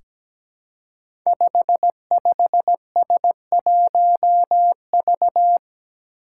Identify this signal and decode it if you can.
Morse code, heard as a keyed tone, about 17 wpm, 700 Hz. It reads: 55S1V
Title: Five Challenging Letter-Number Combinations 17wpm